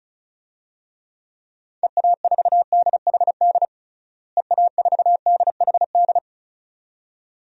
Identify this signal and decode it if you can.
Morse code, heard as a keyed tone, about 36 wpm, 700 Hz. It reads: EA4DHD EA4DHD
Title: DXCC-100 Call Signs - 6 Characters - Repeated 2x - 35wpm